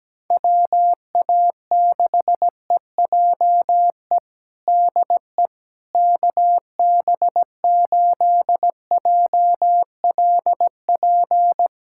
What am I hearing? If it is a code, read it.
Morse code, heard as a keyed tone, about 17 wpm, 700 Hz. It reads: WA6EJE DE KB8JLP
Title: USA Callsign DE Callsign - 6 Characters - 17wpm